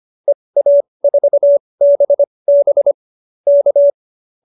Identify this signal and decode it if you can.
Morse code, heard as a keyed tone, about 25 wpm, spaced as for 20 wpm, 590 Hz. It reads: EA4BB K
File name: qso215b 20